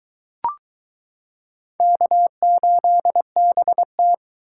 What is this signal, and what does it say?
Morse code, heard as a keyed tone, about 22 wpm, 700 Hz. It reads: K8BT